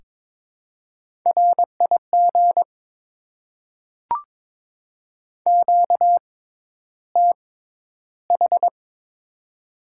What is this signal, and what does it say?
Morse code, heard as a keyed tone, about 23 wpm, spaced as for 11 wpm, 690 Hz. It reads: RIG QTH
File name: Top 20 QSO Elements - 6x Character Spacing - 22wpm